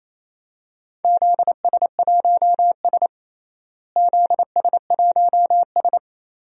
Morse code, heard as a keyed tone, about 27 wpm, 700 Hz. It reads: ZS1S ZS1S
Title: DXCC-100 Call Signs - 4 Characters - Repeated 2x - 28wpm